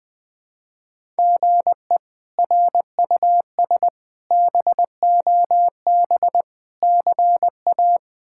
Morse code, heard as a keyed tone, about 20 wpm, 700 Hz. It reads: GE RUS BOB CA
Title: K1USN SST QSO Practice - 20wpm